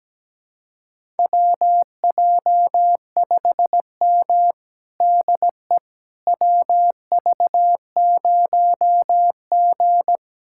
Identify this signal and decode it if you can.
Morse code, heard as a keyed tone, about 17 wpm, 700 Hz. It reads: WJ5M DE WV0G
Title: USA Callsign DE Callsign - 4 Characters - 17wpm